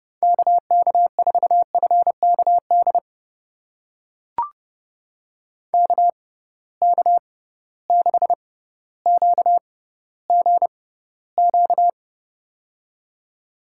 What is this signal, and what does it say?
Morse code, heard as a keyed tone, about 30 wpm, spaced as for 15 wpm, 700 Hz. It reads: KK4FKD KK6QGQ
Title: USA Call Signs - 6 Characters - 6x Character Spacing - 30wpm